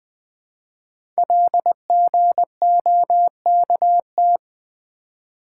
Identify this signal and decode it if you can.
Morse code, heard as a keyed tone, about 20 wpm, 700 Hz. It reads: LGOKT
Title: Code Groups - 5 Letters - 20wpm